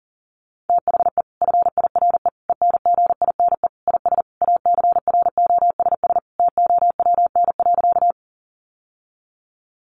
Morse code, heard as a keyed tone, about 37 wpm, 700 Hz. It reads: THE FIRE ENGINE IS ACROSS TOWN.
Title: Sentences from Top 1500 Words 40wpm